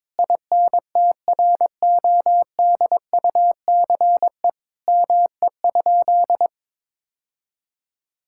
Morse code, heard as a keyed tone, about 21 wpm, 700 Hz. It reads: INTRODUCE ME?
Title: Sentences from Top 1600 Words 22wpm